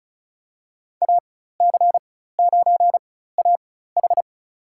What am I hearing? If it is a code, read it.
Morse code, heard as a keyed tone, about 36 wpm, spaced as for 16 wpm, 700 Hz. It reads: AC9AH
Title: USA Call Signs - 5 Characters - 4x Character Spacing - 35wpm